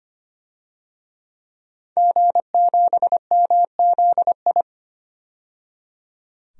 Morse code, heard as a keyed tone, about 25 wpm, 700 Hz. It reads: G7MZI
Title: DXCC-100 Call Signs - 5 Characters - 25wpm